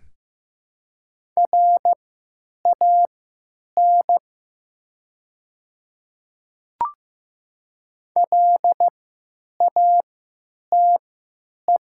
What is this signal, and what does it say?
Morse code, heard as a keyed tone, about 15 wpm, spaced as for 9 wpm, 700 Hz. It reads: RAN LATE
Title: 3x Character Spacing - Top 500 Words - 15wpm